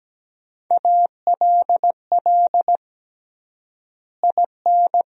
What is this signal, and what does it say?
Morse code, heard as a keyed tone, about 17 wpm, 700 Hz. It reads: ALL IN
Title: 3x Word-Spacing - Sentences from Top 200 Words - 17wpm